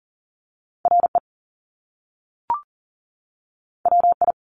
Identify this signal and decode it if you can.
Morse code, heard as a keyed tone, about 38 wpm, 700 Hz. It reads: RE WI
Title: Top Two Letter Combinations in Words 40wpm